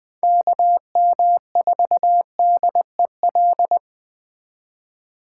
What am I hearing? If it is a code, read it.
Morse code, heard as a keyed tone, about 20 wpm, 700 Hz. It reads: KM4DEL